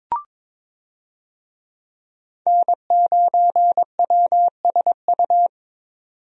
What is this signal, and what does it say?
Morse code, heard as a keyed tone, about 21 wpm, 700 Hz. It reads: N9WSU
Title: USA Call Signs - 5 Characters - 22wpm